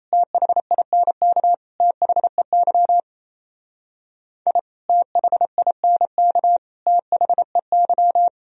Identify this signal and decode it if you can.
Morse code, heard as a keyed tone, about 35 wpm, 700 Hz. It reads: THINK THEY I THINK THEY
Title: Speed-Racing - Sets of 3 Words 28wpm